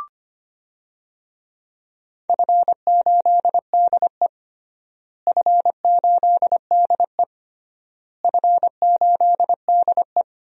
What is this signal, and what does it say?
Morse code, heard as a keyed tone, about 25 wpm, 700 Hz. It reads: F8DE F8DE F8DE
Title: DXCC-100 Call Signs - 4 Characters - Repeated 3x - 25wpm